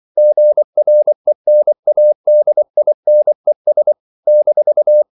Morse code, heard as a keyed tone, about 24 wpm, 600 Hz. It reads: GRENADINES -
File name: CW / Practice / NationalCapitals24